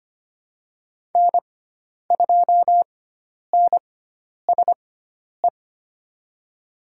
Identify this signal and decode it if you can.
Morse code, heard as a keyed tone, about 25 wpm, spaced as for 10 wpm, 700 Hz. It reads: N2NSE
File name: Lesson 16g - Letter c - Call Signs 25wpm